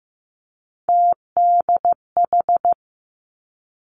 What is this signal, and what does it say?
Morse code, heard as a keyed tone, about 15 wpm, 700 Hz. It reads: TDH
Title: Three Challenging Letter-Number Combinations 15wpm